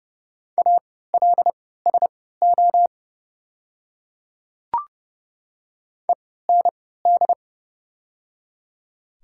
Morse code, heard as a keyed tone, about 30 wpm, spaced as for 17 wpm, 700 Hz. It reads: ALSO END
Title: 3x Character Spacing - Top 200 Words - 30wpm